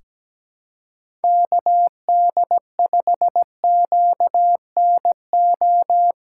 Morse code, heard as a keyed tone, about 17 wpm, 700 Hz. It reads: KD5QNO
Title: USA Callsign DE Callsign 17wpm